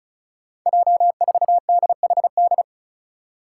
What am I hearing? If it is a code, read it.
Morse code, heard as a keyed tone, about 34 wpm, 700 Hz. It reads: J4DHD
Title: Five Challenging Letter-Number Combinations 35wpm